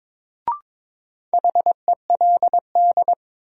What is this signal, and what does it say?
Morse code, heard as a keyed tone, about 23 wpm, 700 Hz. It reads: HELD
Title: Sentences from Top 1600 Words - New Word Drill Alt - 22wpm